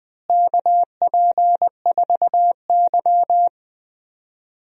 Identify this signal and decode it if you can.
Morse code, heard as a keyed tone, about 21 wpm, 700 Hz. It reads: KP4Y
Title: DXCC-100 Call Signs - 4 Characters - Repeated 3x - 20wpm